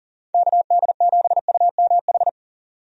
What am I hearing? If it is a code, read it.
Morse code, heard as a keyed tone, about 40 wpm, 700 Hz. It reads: KD7UMH
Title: USA Call Signs - 6 Characters - 40wpm